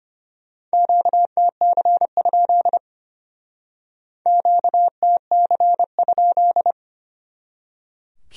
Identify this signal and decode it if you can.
Morse code, heard as a keyed tone, about 32 wpm, 700 Hz. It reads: QTC? QTC?